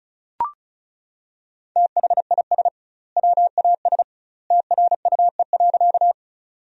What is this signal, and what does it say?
Morse code, heard as a keyed tone, about 36 wpm, 700 Hz. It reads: THIS WAS TRUE.
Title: 2x Word-Spacing - Sentences from Top 400 Words - 35wpm